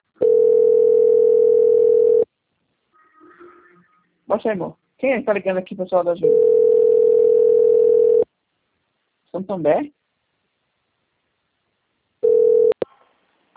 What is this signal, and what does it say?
Ringback tone (North America), 440 + 480 Hz, with speech.
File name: out- - - - - .26743